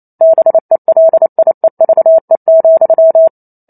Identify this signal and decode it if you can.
Morse code, heard as a keyed tone, about 28 wpm, 650 Hz. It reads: BELIEVE,
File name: Chapter 2 - 28 WPM